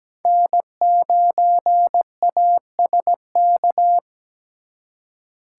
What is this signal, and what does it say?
Morse code, heard as a keyed tone, about 20 wpm, 700 Hz. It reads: N9ASK
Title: Speed Racing - USA Call Signs 17wpm